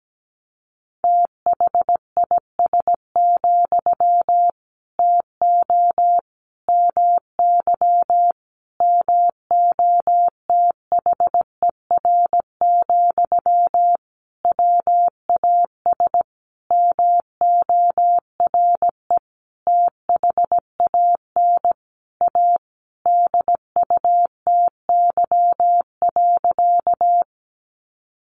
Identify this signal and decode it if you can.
Morse code, heard as a keyed tone, about 17 wpm, 700 Hz. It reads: THIS, TO MY MOTHER, WAS MORE THAN A DUTY.
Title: Sentences from Top 1500 Words 17wpm